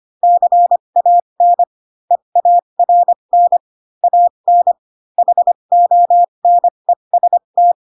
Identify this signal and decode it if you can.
Morse code, heard as a keyed tone, about 25 wpm, spaced as for 22 wpm, 700 Hz. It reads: CAN EARN AN HONEST
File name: Pinocchio / Pinocchio ch4 18 25 ik1hge